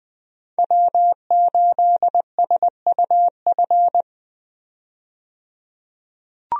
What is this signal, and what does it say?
Morse code, heard as a keyed tone, about 20 wpm, 700 Hz. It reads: W8SUF
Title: USA Call Signs - 5 Characters - 20wpm